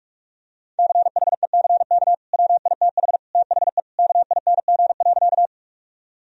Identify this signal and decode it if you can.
Morse code, heard as a keyed tone, about 41 wpm, 700 Hz. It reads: CHECK WITH THE KING.
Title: Sentences from Top 1000 Words - Review Part 3 - 45wpm